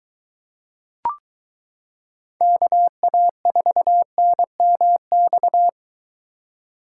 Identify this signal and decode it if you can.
Morse code, heard as a keyed tone, about 23 wpm, 700 Hz. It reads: KA4NMX